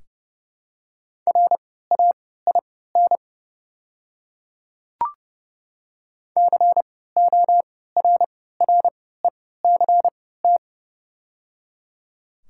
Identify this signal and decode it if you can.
Morse code, heard as a keyed tone, about 31 wpm, spaced as for 17 wpm, 700 Hz. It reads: RAIN CORRECT
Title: 3x Character Spacing - Top 500 Words - 30wpm